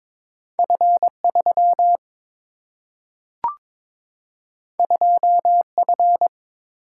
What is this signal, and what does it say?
Morse code, heard as a keyed tone, about 22 wpm, 700 Hz. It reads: F 3 2 F
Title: Single-Letter Single-Number 22wpm